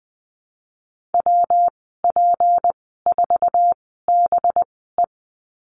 Morse code, heard as a keyed tone, about 20 wpm, spaced as for 14 wpm, 700 Hz. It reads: WP4BE
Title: 193 Lesson 31g - Letter v - Call Signs 20wpm